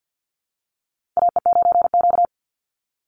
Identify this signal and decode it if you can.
Morse code, heard as a keyed tone, about 50 wpm, 700 Hz. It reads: AE9Q
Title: NA SOTA Chaser Callsigns 50wpm